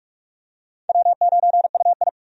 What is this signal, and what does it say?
Morse code, heard as a keyed tone, about 45 wpm, 700 Hz. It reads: W9UI
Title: USA Call Signs - 4 Characters - 45wpm